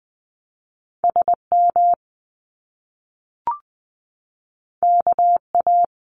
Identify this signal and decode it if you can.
Morse code, heard as a keyed tone, about 20 wpm, 700 Hz. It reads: SM KA
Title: Top Two Letter Combinations in Words 20wpm